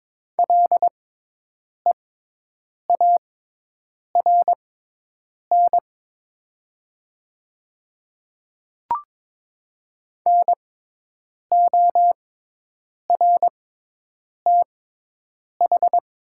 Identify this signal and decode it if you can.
Morse code, heard as a keyed tone, about 22 wpm, spaced as for 8 wpm, 700 Hz. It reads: LEARN NORTH
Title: 6x Character Spacing - Top 300 Words - 22wpm